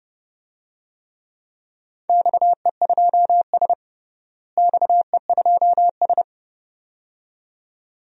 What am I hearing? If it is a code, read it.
Morse code, heard as a keyed tone, about 30 wpm, 700 Hz. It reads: XE2S XE2S
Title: DXCC-100 Call Signs - 4 Characters - Repeated 2x - 30wpm